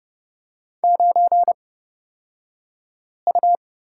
Morse code, heard as a keyed tone, about 29 wpm, spaced as for 5 wpm, 690 Hz. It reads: 9U
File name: Lesson 36d - BT - Two Characters 30wpm